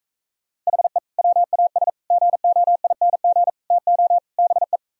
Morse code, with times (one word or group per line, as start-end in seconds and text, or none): 0.7-1.0 HE
1.2-1.9 WAS
2.1-3.5 GOING
3.7-4.2 TO
4.4-4.8 BE